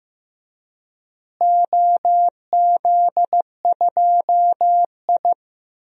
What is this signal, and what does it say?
Morse code, heard as a keyed tone, about 16 wpm, 700 Hz. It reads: OZ2I
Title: DXCC-100 Call Signs - 4 Characters - Repeated 3x - 15wpm